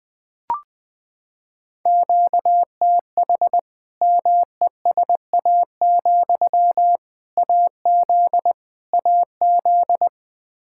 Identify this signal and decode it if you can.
Morse code, heard as a keyed tone, about 20 wpm, 700 Hz. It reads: QTH MESA, AZ AZ
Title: Practice QSO - Exchange 1 - Beginning - 20wpm